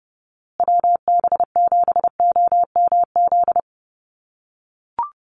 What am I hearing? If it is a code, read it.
Morse code, heard as a keyed tone, about 30 wpm, 700 Hz. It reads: WB7OMZ
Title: USA Call Signs Repeated 2x 30wpm